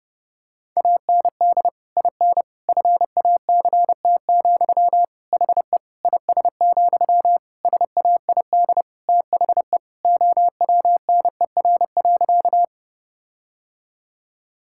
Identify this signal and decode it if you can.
Morse code, heard as a keyed tone, about 30 wpm, 700 Hz. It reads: AND IN FACT, HE IS, SAID THE OWNER.
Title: Sentences from Top 1600 Words 30wpm